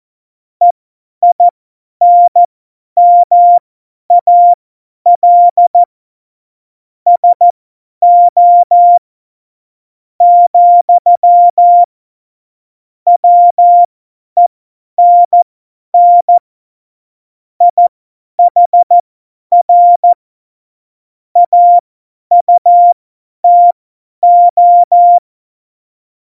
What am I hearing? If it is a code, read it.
Morse code, heard as a keyed tone, about 14 wpm, spaced as for 10 wpm, 700 Hz. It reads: EINMAL SO, WENN IHR AUTO
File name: Morsekurs / Witze / Witz 26. Februar 2019, 50 F70